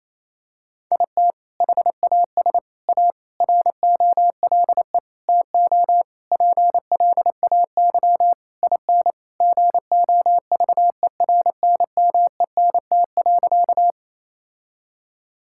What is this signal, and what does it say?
Morse code, heard as a keyed tone, about 28 wpm, 700 Hz. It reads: IT HAS A ROLE TO PLAY IN GOVERNMENT.
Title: Sentences from Top 1000 Words - Review Part 6 - 28wpm